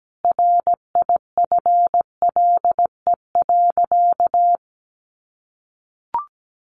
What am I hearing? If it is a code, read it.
Morse code, heard as a keyed tone, about 17 wpm, 700 Hz. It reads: RIFLE.
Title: A Sound Of Thunder 17wpm